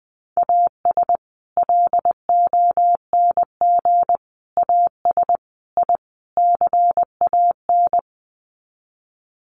Morse code, heard as a keyed tone, about 20 wpm, 700 Hz. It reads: AS LONG AS I CAN
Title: Sets of 5 Words Encore2 20wpm